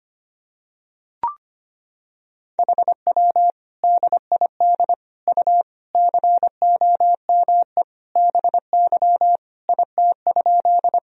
Morse code, heard as a keyed tone, about 25 wpm, 700 Hz. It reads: HW DID U COME BY IT?
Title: Sentences from Top 100 Words and Abbreviations 25wpm